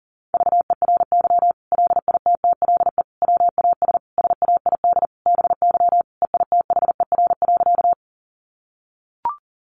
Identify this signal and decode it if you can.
Morse code, heard as a keyed tone, about 40 wpm, 700 Hz. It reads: VERY LITTLE WAS SAID BY EITHER.
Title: Sentences from Top 600 Words 40wpm